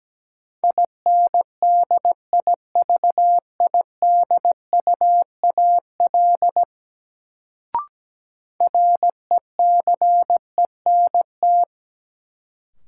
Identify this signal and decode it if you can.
Morse code, heard as a keyed tone, about 17 wpm, 700 Hz. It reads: INDIVIDUAL RECENT
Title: Sentences from Top 1400 Words - New Word Drill Alt - 17wpm